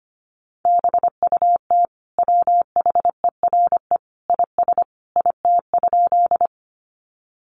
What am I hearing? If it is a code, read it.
Morse code, heard as a keyed tone, about 25 wpm, 700 Hz. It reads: BUT WHERE IS IT?
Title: Sentences from Top 200 Words 25wpm